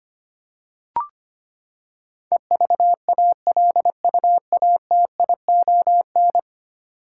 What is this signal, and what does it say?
Morse code, heard as a keyed tone, about 25 wpm, 700 Hz. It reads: EVALUATION
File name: Ten Letter Words - 25wpm